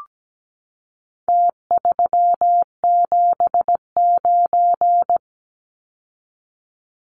Morse code, heard as a keyed tone, about 17 wpm, 700 Hz. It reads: T379
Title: Numbers - Contest Style - 4-Digit 17wpm